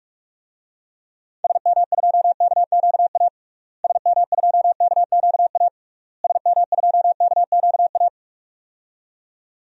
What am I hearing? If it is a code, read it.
Morse code, heard as a keyed tone, about 41 wpm, 700 Hz. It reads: SM2KQA SM2KQA SM2KQA
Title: DXCC-100 Call Signs - 6 Characters - Repeated 3x - 45wpm